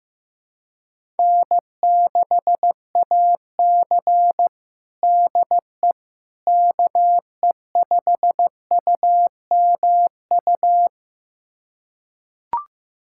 Morse code, heard as a keyed tone, about 15 wpm, 700 Hz. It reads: N6AC DE KE5UMU
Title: USA Callsign DE Callsign 15wpm